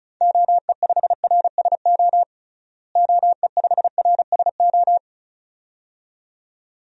Morse code, heard as a keyed tone, about 36 wpm, 700 Hz. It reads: OE5RSO OE5RSO
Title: DXCC-100 Call Signs - 6 Characters - Repeated 2x - 35wpm